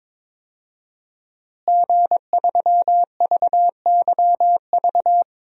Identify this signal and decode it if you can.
Morse code, heard as a keyed tone, about 22 wpm, 700 Hz. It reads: G3VYV
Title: DXCC-100 Call Signs - 5 Characters - Repeated 3x - 22wpm